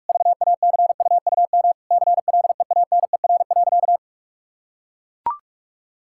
Morse code, heard as a keyed tone, about 41 wpm, 700 Hz. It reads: VACUUM CLEANER.
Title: Pangram Challenge v2 - 45wpm